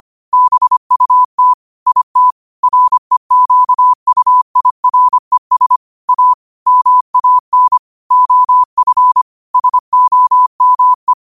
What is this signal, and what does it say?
Morse code, heard as a keyed tone, about 25 wpm, 1000 Hz. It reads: BUT IT REQUIRES A MAN OF SOME
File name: qotdfiles / 25wpm